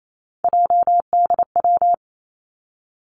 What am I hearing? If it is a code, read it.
Morse code, heard as a keyed tone, about 28 wpm, 700 Hz. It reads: JDW
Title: Three Challenging Letter-Number Combinations 28wpm